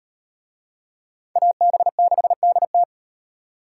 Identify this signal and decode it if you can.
Morse code, heard as a keyed tone, about 38 wpm, 700 Hz. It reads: AB6DT